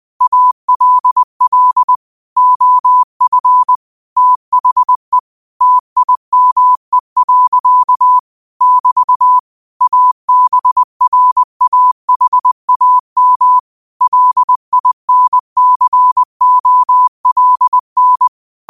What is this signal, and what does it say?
Morse code, heard as a keyed tone, about 20 wpm, 1000 Hz. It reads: ALL OF THE TIME. <BT> ABRAHAM LINCOLN
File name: qotdfiles / 20wpm